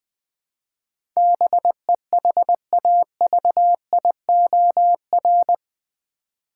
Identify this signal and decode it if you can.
Morse code, heard as a keyed tone, about 21 wpm, 700 Hz. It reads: BEHAVIOR
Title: Build A Word - 4 Letters At A Time - 10-Letter Words - 20wpm